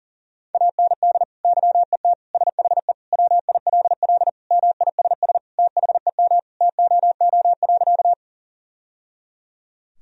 Morse code, heard as a keyed tone, about 40 wpm, 700 Hz. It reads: AND YET SHE WILL MISS THEM TOO.
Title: Sentences from Top 1000 Words - Review Part 3 - 40wpm